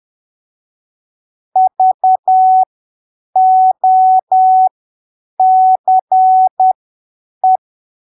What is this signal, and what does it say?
Morse code, heard as a keyed tone, about 10 wpm, spaced as for 7 wpm, 750 Hz. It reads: VOCE
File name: arquivos / texto / texto 16 10ppm